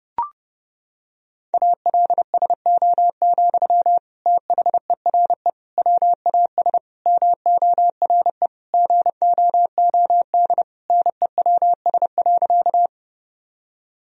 Morse code, heard as a keyed tone, about 32 wpm, 700 Hz. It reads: ALSO, THERE WAS MORE GOOD NEWS.
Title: Sentences from Top 1000 Words - Review Part 1 - 30wpm